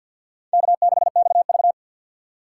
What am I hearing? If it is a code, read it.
Morse code, heard as a keyed tone, about 52 wpm, 700 Hz. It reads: K6XV
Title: USA Call Signs - 4 Characters - 3x Character Spacing - 50wpm